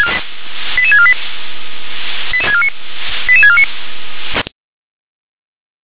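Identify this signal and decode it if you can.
Five-tone selective call (ZVEI), heard as quick runs of five tones, tones of about 70 ms. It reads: 88648 88649
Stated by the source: ZVEI five-tone selective call received on SDR.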